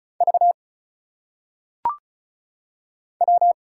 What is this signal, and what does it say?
Morse code, heard as a keyed tone, about 36 wpm, 700 Hz. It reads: V W
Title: Single Letters - Phonetic Pronunciation 35wpm